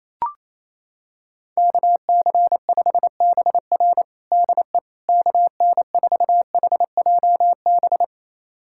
Morse code, heard as a keyed tone, about 28 wpm, 700 Hz. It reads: KC5BR DE KN4HJB
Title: USA Callsign DE Callsign 28wpm